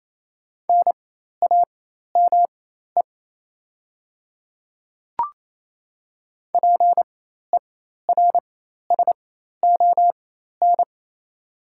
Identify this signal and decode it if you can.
Morse code, heard as a keyed tone, about 28 wpm, spaced as for 13 wpm, 700 Hz. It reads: NAME PERSON
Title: 4x Character Spacing - Top 500 Words - 28wpm